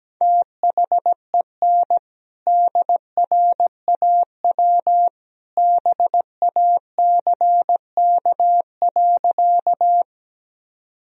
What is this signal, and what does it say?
Morse code, heard as a keyed tone, about 17 wpm, 700 Hz. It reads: THEN DRAW BACK.
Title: Sentences from Top 1000 Words - Review Part 1 - 17wpm